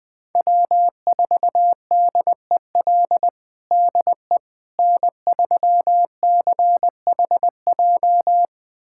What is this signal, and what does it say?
Morse code, heard as a keyed tone, about 20 wpm, 700 Hz. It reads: W4DEL DE N3CHJ
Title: USA Callsign DE Callsign - 5 Characters - 20wpm